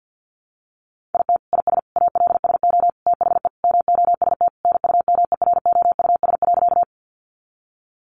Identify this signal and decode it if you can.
Morse code, heard as a keyed tone, about 50 wpm, 700 Hz. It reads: IT IS ALSO THE MOST NUMEROUS.
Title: Sentences from Top 1400 Words 50wpm